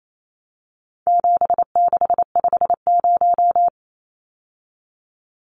Morse code, heard as a keyed tone, about 28 wpm, 700 Hz. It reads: 7650
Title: Numbers - 4-Digit 28wpm